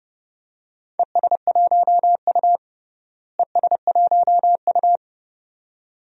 Morse code, heard as a keyed tone, about 32 wpm, 700 Hz. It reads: ES1U ES1U
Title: DXCC-100 Call Signs - 4 Characters - Repeated 2x - 30wpm